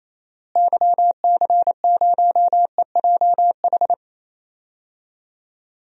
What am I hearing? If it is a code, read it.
Morse code, heard as a keyed tone, about 28 wpm, 700 Hz. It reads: YC0EJH
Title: DXCC-100 Call Signs - 6 Characters - Repeated 3x - 28wpm